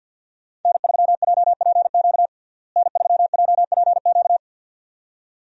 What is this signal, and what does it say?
Morse code, heard as a keyed tone, about 50 wpm, 700 Hz. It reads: N3JPQ N3JPQ
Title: USA Call Signs - 5 Characters - Repeated 2x - 50wpm